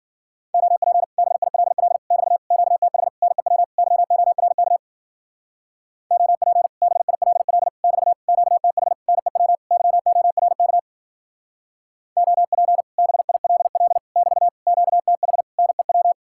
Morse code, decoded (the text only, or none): OP BILL <BT> QTH NEW YORK OP BILL <BT> QTH NEW YORK OP BILL <BT> QTH NEW